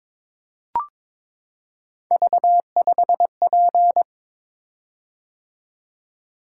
Morse code, heard as a keyed tone, about 22 wpm, 700 Hz. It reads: V5P
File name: practice / 22wpm / Three Letter-Number Combinations - 22wpm